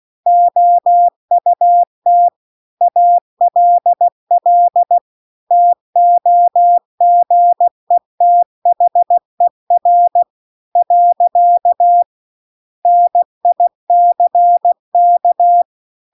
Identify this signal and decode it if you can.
Morse code, heard as a keyed tone, about 16 wpm, 700 Hz. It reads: OUT ALL TOGETHER . NICK
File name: CW-The-Big-Twoheaded-River-2-16WPM